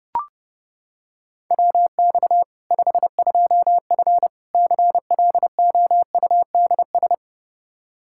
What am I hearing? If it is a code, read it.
Morse code, heard as a keyed tone, about 30 wpm, 700 Hz. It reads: WX 52F CLOUDS
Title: K1USN SST QSO Practice 30wpm